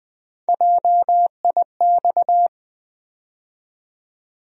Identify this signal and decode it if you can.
Morse code, heard as a keyed tone, about 20 wpm, 700 Hz. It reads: JIX
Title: USA Call Sign Postfixes - 3 Letters 20wpm